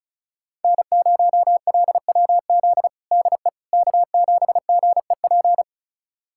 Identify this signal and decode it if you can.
Morse code, heard as a keyed tone, about 36 wpm, 690 Hz. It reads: N0LWZ DE K7GEP
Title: USA Callsign DE Callsign - 5 Characters - 35wpm